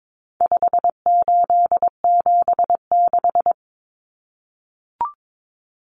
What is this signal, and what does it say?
Morse code, heard as a keyed tone, about 22 wpm, 700 Hz. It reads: 5876
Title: Numbers - 4-Digit 22wpm